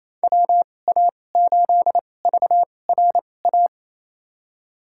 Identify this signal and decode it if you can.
Morse code, heard as a keyed tone, about 28 wpm, spaced as for 20 wpm, 700 Hz. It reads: WA8VRA
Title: USA Call Signs - 6 Characters - 2x Character Spacing - 28wpm